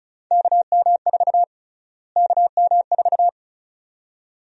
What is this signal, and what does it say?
Morse code, heard as a keyed tone, about 36 wpm, 700 Hz. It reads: KM4 KM4
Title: USA Call Sign Prefixes - Repeated 2x - 35wpm